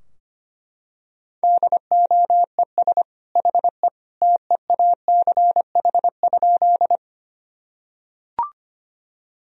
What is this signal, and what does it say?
Morse code, heard as a keyed tone, about 25 wpm, 700 Hz. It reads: DOES HE TEACH?
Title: Sentences from Top 1000 Words - Review Part 2 - 25wpm